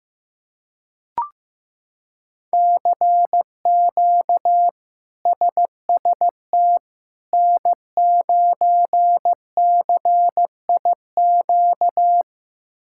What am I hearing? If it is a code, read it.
Morse code, heard as a keyed tone, about 15 wpm, 700 Hz. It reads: CQ SST N9CIQ
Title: K1USN SST QSO Practice - 15wpm